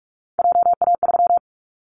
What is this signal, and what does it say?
Morse code, heard as a keyed tone, about 46 wpm, 700 Hz. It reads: JA3
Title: Callsign Prefixes for Top 25 DXCC - 45wpm